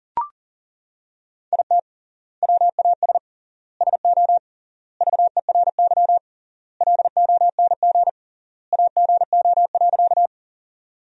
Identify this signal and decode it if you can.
Morse code, heard as a keyed tone, about 40 wpm, 700 Hz. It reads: IT WAS SO VERY LONG AGO.
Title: 3x Word-Spacing - Sentences from Top 500 Words - 40wpm